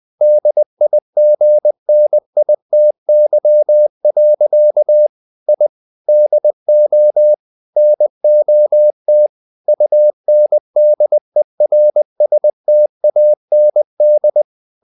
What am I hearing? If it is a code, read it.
Morse code, heard as a keyed tone, about 20 wpm, 600 Hz. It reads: DIGNITY. I DO NOT UNDERSTAND
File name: the old man and the sea part 2 20wpm